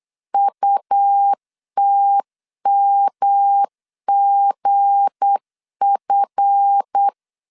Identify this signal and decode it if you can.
Morse code, heard as a keyed tone, about 8 wpm, 800 Hz. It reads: UTMGF
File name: LEZIONI / 7 GF9Z / 7 GF 8wpm 3108